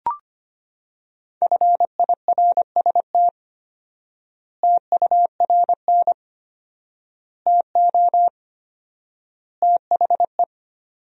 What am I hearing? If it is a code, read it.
Morse code, heard as a keyed tone, about 25 wpm, 700 Hz. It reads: FIRST TURN TO THE
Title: 4x Word-Spacing - Sentences from Top 300 Words - 25wpm